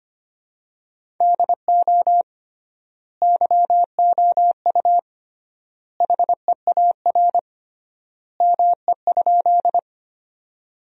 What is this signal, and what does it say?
Morse code, heard as a keyed tone, about 25 wpm, 700 Hz. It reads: DO YOU HEAR ME?
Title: 3x Word-Spacing - Sentences from Top 300 Words - 25wpm